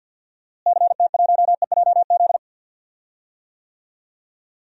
Morse code, heard as a keyed tone, about 50 wpm, 700 Hz. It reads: CT1EJZ
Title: DXCC-100 Call Signs - 6 Characters - Repeated 2x - 50wpm